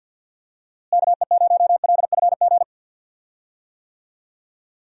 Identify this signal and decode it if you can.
Morse code, heard as a keyed tone, about 52 wpm, 700 Hz. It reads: KE0LFG
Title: USA Call Signs - 6 Characters - Repeated 2x - 50wpm